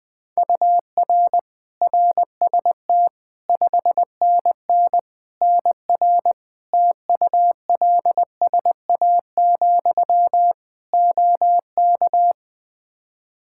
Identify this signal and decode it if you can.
Morse code, heard as a keyed tone, about 20 wpm, 700 Hz. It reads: UR RST 5NN NR TULSA, OK
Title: Practice QSO - Exchange 1 - Beginning - 20wpm